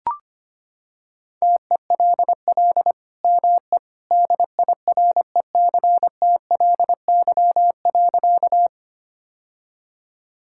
Morse code, heard as a keyed tone, about 25 wpm, 700 Hz. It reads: TELL ME DIRECTLY.
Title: Sentences from Top 1000 Words - Review Part 5 - 25wpm